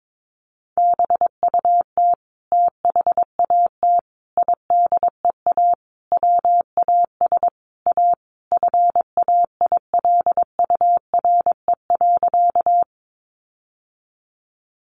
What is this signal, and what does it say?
Morse code, heard as a keyed tone, about 22 wpm, 700 Hz. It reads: BUT THAT IDEA WAS A FAILURE.
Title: Sentences from Top 900 Words 22wpm